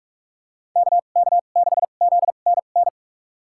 Morse code, heard as a keyed tone, about 42 wpm, spaced as for 31 wpm, 700 Hz. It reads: KK6ZNN
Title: USA Call Signs - 6 Characters - 2x Character Spacing - 45wpm